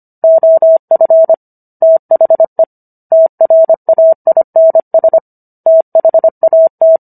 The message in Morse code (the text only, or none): OF THE TRAINS THAT